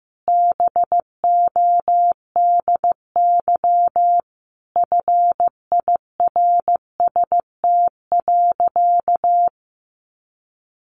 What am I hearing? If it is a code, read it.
Morse code, heard as a keyed tone, about 15 wpm, 700 Hz. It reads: BODY FIRST.
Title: Sentences from Top 1400 Words 15wpm